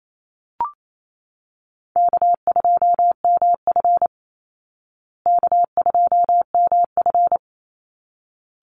Morse code, heard as a keyed tone, about 28 wpm, 700 Hz. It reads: K2MF K2MF
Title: Easy Call Signs Repeated 2x - 28wpm